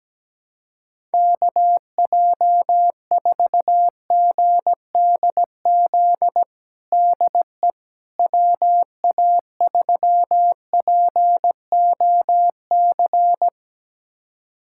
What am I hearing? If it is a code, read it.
Morse code, heard as a keyed tone, about 17 wpm, 700 Hz. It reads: KJ4GDZ DE WA3POC
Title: USA Callsign DE Callsign 17wpm